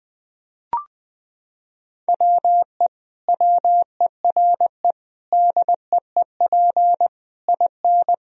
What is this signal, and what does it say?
Morse code, heard as a keyed tone, about 19 wpm, 700 Hz. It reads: WE WERE DEEP IN
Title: Sentences from Top 1000 Words - Review Part 3 - 20wpm